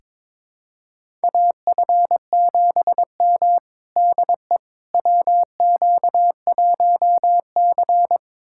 Morse code, heard as a keyed tone, about 22 wpm, 700 Hz. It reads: AF7M DE WQ1C
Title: USA Callsign DE Callsign - 4 Characters - 22wpm